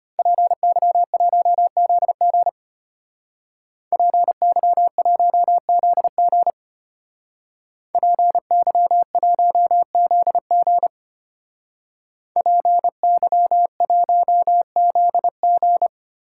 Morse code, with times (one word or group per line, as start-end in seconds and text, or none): 0.2-2.5 PY1ZG
3.9-6.5 PY1ZG
7.9-10.9 PY1ZG
12.4-15.9 PY1ZG